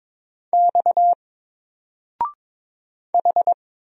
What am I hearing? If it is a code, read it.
Morse code, heard as a keyed tone, about 22 wpm, 700 Hz. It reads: X H
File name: Single-Letter-Number-22wpm